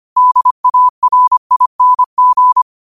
Morse code, heard as a keyed tone, about 25 wpm, 1000 Hz. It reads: DARING
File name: qotdfiles / 25wpm